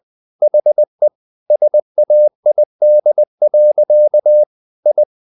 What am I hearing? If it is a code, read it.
Morse code, heard as a keyed tone, about 20 wpm, 600 Hz. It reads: HE SAID. I
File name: the old man and the sea part 2 20wpm